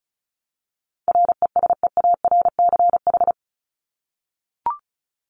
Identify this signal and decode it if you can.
Morse code, heard as a keyed tone, about 34 wpm, 700 Hz. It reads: RESEARCH
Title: Top 501-750 Words 35wpm